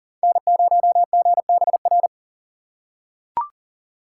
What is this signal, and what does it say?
Morse code, heard as a keyed tone, about 40 wpm, 700 Hz. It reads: N0GBR
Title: USA Call Signs - 6x Character Spacing - 40wpm